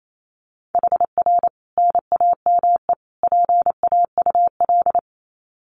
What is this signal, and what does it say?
Morse code, heard as a keyed tone, about 28 wpm, 700 Hz. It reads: HR NAME PAUL
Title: Common QSO Phrases 28wpm